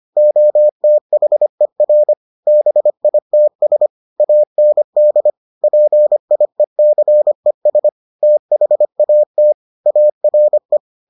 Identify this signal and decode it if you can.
Morse code, heard as a keyed tone, about 25 wpm, 600 Hz. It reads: OTHER BITS AND PIECES THAT ARE